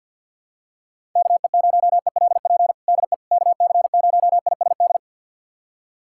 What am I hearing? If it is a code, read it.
Morse code, heard as a keyed tone, about 51 wpm, 700 Hz. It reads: KE0ELP DE KC0ISD
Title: USA Callsign DE Callsign - 6 Characters - 50wpm